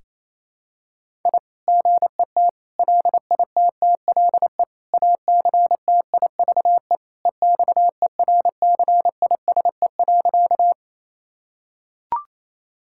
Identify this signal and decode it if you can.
Morse code, heard as a keyed tone, about 28 wpm, 700 Hz. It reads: I GET LITTLE ACTIVE EXERCISE.
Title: Sentences from Top 1000 Words - Review Part 6 - 28wpm